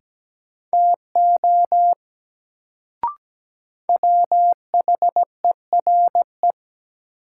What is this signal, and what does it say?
Morse code, heard as a keyed tone, about 17 wpm, 700 Hz. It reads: TO WHERE
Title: Sets of 2 Words - New Word Drill - 17wpm